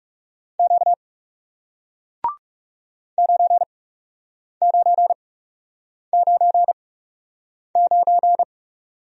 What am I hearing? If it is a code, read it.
Morse code, heard as a keyed tone, about 40 wpm, spaced as for 8 wpm, 700 Hz. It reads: Q 9999